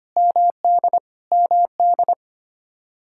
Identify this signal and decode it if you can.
Morse code, heard as a keyed tone, about 25 wpm, 700 Hz. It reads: MD MD
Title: Practice QSO - Exchange 1 - Beginning - 25wpm